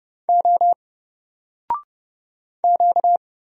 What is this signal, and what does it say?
Morse code, heard as a keyed tone, about 30 wpm, 700 Hz. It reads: O Q
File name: Single-Letter-Number-30wpm